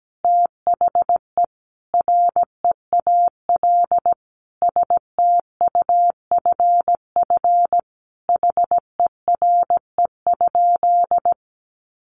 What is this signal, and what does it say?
Morse code, heard as a keyed tone, about 17 wpm, 700 Hz. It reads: THE REAL STUFF HERE?
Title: Sentences from Top 1400 Words 17wpm